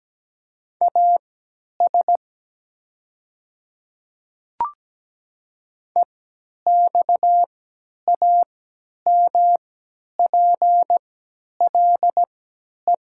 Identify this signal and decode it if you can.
Morse code, heard as a keyed tone, about 17 wpm, spaced as for 10 wpm, 690 Hz. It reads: AS EXAMPLE
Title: 3x Character Spacing - Top 300 Words - 17wpm